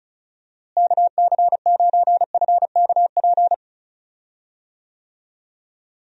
Morse code, heard as a keyed tone, about 36 wpm, 700 Hz. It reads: KC9FKP
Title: USA Call Signs - 6 Characters - Repeated 2x - 35wpm